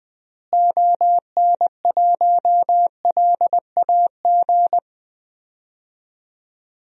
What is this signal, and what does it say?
Morse code, heard as a keyed tone, about 20 wpm, 700 Hz. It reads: ON1LAG
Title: DXCC-100 Call Signs - 6 Characters - Repeated 3x - 20wpm